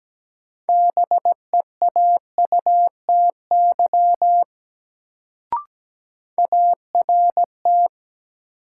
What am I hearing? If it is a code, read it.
Morse code, heard as a keyed tone, about 17 wpm, 700 Hz. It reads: BEAUTY ART
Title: Sentences from Top 600 Words - New Word Drill - 17wpm